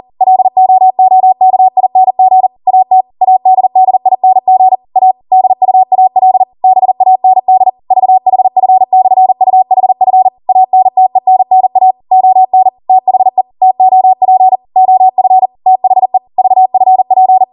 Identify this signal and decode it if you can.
Morse code, heard as a keyed tone, about 40 wpm, 750 Hz. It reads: LOOKING AT ADDING A DUAL BAND VHF/UHF ANTENNA ON THE TOP OF THE VFP
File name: Morse / 40WPM